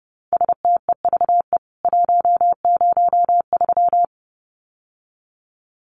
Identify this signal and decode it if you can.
Morse code, heard as a keyed tone, about 30 wpm, 700 Hz. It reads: STEVE 103
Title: CWT Exchange 30wpm